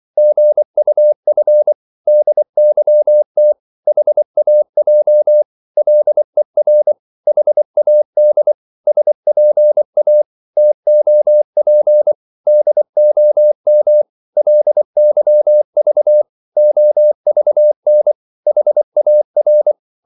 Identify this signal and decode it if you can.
Morse code, heard as a keyed tone, about 24 wpm, 600 Hz. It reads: GUF DYT HAJ LER HAD SPA TOP DOM LYV OVN HAR
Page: Korte danske ord 24wpm